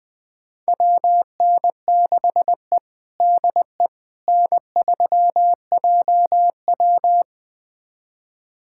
Morse code, heard as a keyed tone, about 21 wpm, 700 Hz. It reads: WN6E DE N3JW
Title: USA Callsign DE Callsign - 4 Characters - 20wpm